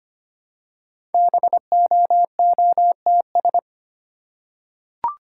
Morse code, heard as a keyed tone, about 25 wpm, 690 Hz. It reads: BOOTS
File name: practice / 25wpm / Easy Words with Hints - 25wpm